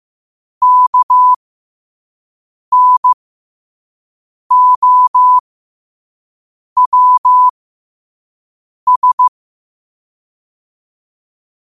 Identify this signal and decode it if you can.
Morse code, heard as a keyed tone, about 15 wpm, spaced as for 5 wpm, 1000 Hz. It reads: KNOWS
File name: qotdfiles / 5wpm